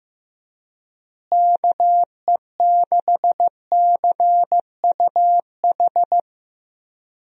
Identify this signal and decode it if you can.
Morse code, heard as a keyed tone, about 16 wpm, 700 Hz. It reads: KE6CUH
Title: USA Call Signs - 6 Characters - Repeated 3x - 15wpm